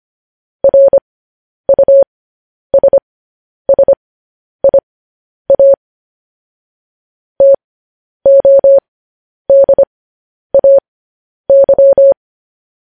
Morse code, heard as a keyed tone, about 25 wpm, spaced as for 10 wpm, 550 Hz. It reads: RUSSIA TODAY